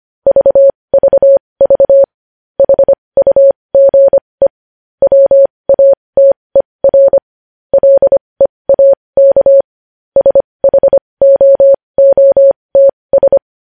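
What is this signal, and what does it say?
Morse code, heard as a keyed tone, about 25 wpm, spaced as for 20 wpm, 550 Hz. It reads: VVV HUGE WATER LEAK SHOOTS
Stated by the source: Morse code news headlines sent at 20 wpm.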